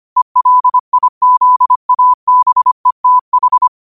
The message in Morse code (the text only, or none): ELIZABETH